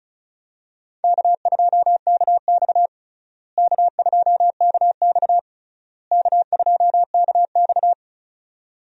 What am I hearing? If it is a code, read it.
Morse code, heard as a keyed tone, about 36 wpm, 700 Hz. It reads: K2KX K2KX K2KX
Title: USA Call Signs - 4 Characters - Repeated 3x - 35wpm